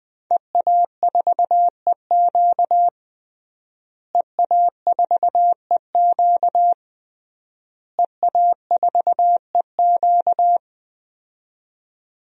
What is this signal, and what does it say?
Morse code, heard as a keyed tone, about 20 wpm, 700 Hz. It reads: EA4EQ EA4EQ EA4EQ
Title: DXCC-100 Call Signs - 5 Characters - Repeated 3x - 20wpm